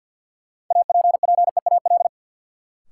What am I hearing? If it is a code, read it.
Morse code, heard as a keyed tone, about 52 wpm, 700 Hz. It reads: APPEAL
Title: Sentences from Top 1300 Words - New Word Drill - 50wpm